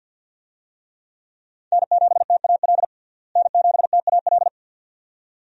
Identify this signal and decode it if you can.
Morse code, heard as a keyed tone, about 51 wpm, 700 Hz. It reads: N7TAL N7TAL
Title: USA Call Signs - 5 Characters - Repeated 2x - 50wpm